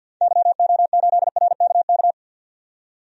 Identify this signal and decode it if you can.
Morse code, heard as a keyed tone, about 52 wpm, 700 Hz. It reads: YO8RKK
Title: DXCC-100 Call Signs - 6 Characters - 50wpm